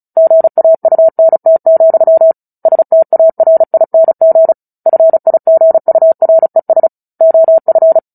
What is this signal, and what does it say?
Morse code, heard as a keyed tone, about 34 wpm, 650 Hz. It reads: GAUNT, STARING FIGURES OF
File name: Book Two / 09-Chapter 9 - 35 WPM